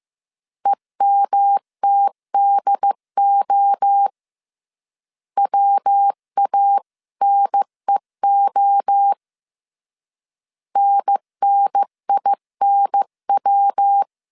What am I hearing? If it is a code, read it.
Morse code, heard as a keyed tone, about 15 wpm, 800 Hz. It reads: EMTDO WANEO NNINW
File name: CW / DET / 1 3 1500C S C 14wpm 2458